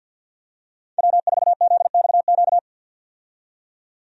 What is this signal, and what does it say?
Morse code, heard as a keyed tone, about 47 wpm, 700 Hz. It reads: W4ZXQ
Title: USA Call Signs - 5 Characters - 50wpm